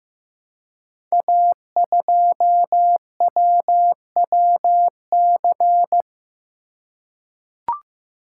Morse code, heard as a keyed tone, about 15 wpm, 700 Hz. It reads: A2WWC
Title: Code Groups - 5 Letter:Numbers - 15wpm